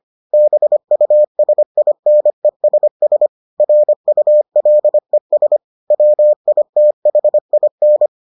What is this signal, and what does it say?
Morse code, heard as a keyed tone, about 25 wpm, 600 Hz. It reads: BUSINESS RULES WITHIN